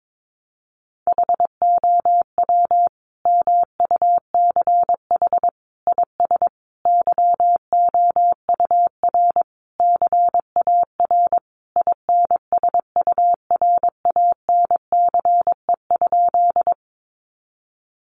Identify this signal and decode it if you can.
Morse code, heard as a keyed tone, about 22 wpm, 700 Hz. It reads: HOW MUCH IS YOUR CAR INSURANCE?
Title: Sentences from Top 1100 Words 22wpm